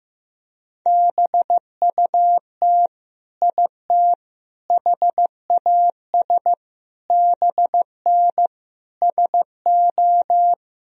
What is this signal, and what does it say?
Morse code, heard as a keyed tone, about 16 wpm, 700 Hz. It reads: BUT IT HAS BN SO
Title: Sentences from Top 100 Words and Abbreviations 15wpm